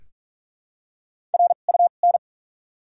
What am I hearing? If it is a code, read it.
Morse code, heard as a keyed tone, about 42 wpm, spaced as for 31 wpm, 700 Hz. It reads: RUN
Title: 2x Character Spacing - Top 500 Words - 45wpm